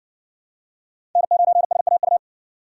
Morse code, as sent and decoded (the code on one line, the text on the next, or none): -. ----. ... .- ..-
N9SAU